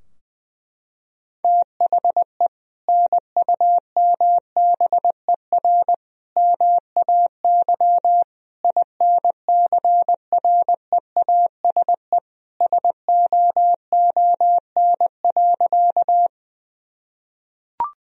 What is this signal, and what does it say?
Morse code, heard as a keyed tone, about 21 wpm, 700 Hz. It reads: THE NUMBER MAY INCREASE SOON.
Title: Sentences from Top 1000 Words - Review Part 4 - 20wpm